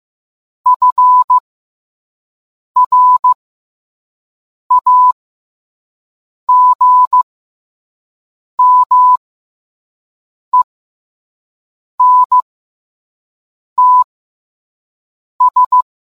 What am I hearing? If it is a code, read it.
Morse code, heard as a keyed tone, about 15 wpm, spaced as for 5 wpm, 1000 Hz. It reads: FRAGMENTS